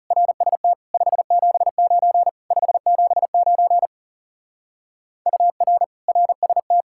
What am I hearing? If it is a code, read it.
Morse code, heard as a keyed tone, about 36 wpm, 700 Hz. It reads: RST 579 579 UR RST
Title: Speed-Racing Common QSO Phrases - 30wpm